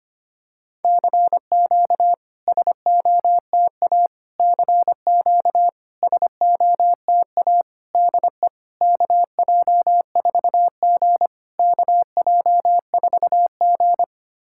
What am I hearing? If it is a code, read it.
Morse code, heard as a keyed tone, about 25 wpm, 700 Hz. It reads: CQ SOTA CQ SOTA DE KJ4G KJ4G
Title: SOTA QSO Practice 25wpm